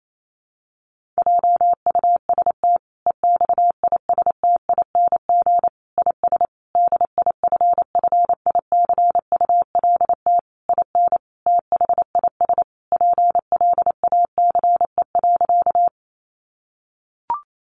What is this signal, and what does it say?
Morse code, heard as a keyed tone, about 28 wpm, 700 Hz. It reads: JUST EXISTING IS DIFFICULT IN THIS PLACE.
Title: Sentences from Top 1500 Words 28wpm